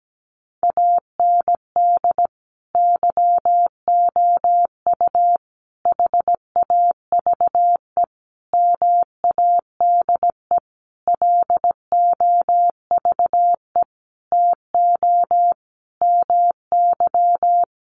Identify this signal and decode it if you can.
Morse code, heard as a keyed tone, about 17 wpm, 700 Hz. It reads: AND YOU HAVE MADE LOVE TO MY
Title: Sentences from Top 1500 Words 17wpm